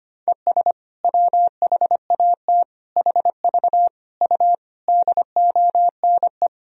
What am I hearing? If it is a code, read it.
Morse code, heard as a keyed tone, about 25 wpm, 700 Hz. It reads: ES WHAT HV U DONE
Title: Sentences from Top 500 Words and Abbreviations 25wpm